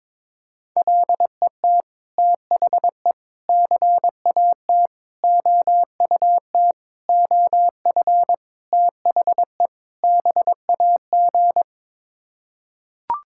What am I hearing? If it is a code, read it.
Morse code, heard as a keyed tone, about 22 wpm, 700 Hz. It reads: LET THE CAT OUT OF THE BAG
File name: practice / 22wpm / Sets of 7 Words - 22wpm